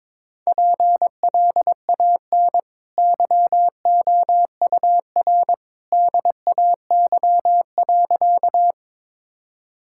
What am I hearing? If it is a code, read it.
Morse code, heard as a keyed tone, about 22 wpm, 690 Hz. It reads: PLAN YOUR DAY.
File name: practice / 22wpm / Easy Sentences with Hints - 22wpm